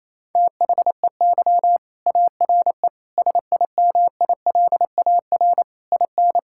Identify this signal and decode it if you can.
Morse code, heard as a keyed tone, about 28 wpm, 700 Hz. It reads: THEY ARE SIMILAR IN
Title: Sentences from Top 1000 Words - Review Part 5 - 28wpm